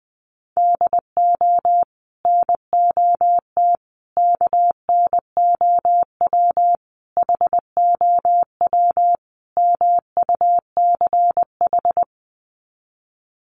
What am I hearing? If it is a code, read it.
Morse code, heard as a keyed tone, about 20 wpm, 700 Hz. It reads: DO NOT KNOW HOW MUCH
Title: Sets of 5 Words Encore 20wpm